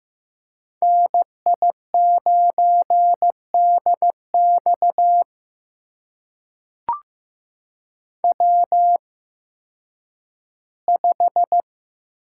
Morse code, heard as a keyed tone, about 15 wpm, spaced as for 10 wpm, 700 Hz. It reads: NI9DX W5
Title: USA Call Signs - 5 Characters - 8x Character Spacing - 15wpm